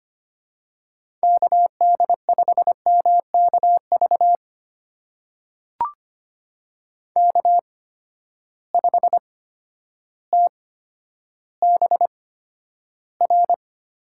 Morse code, heard as a keyed tone, about 25 wpm, spaced as for 11 wpm, 700 Hz. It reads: KD5MKV K5TBR
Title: USA Call Signs - 8x Character Spacing - 25wpm